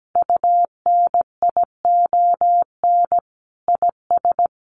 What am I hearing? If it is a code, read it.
Morse code, heard as a keyed tone, about 17 wpm, 700 Hz. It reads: UNION IS
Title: Sentences from Top 1400 Words 17wpm